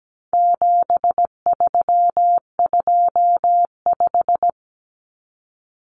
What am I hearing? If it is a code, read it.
Morse code, heard as a keyed tone, about 17 wpm, 700 Hz. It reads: 7325
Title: Numbers - 4-Digit 17wpm